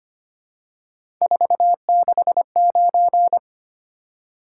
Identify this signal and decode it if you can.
Morse code, heard as a keyed tone, about 25 wpm, 700 Hz. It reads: 469
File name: Three Letter-Number Combinations - 25wpm